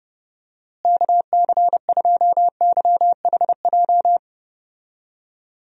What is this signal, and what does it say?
Morse code, heard as a keyed tone, about 32 wpm, 700 Hz. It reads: KC2YHJ
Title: USA Call Signs - 6 Characters - Repeated 3x - 30wpm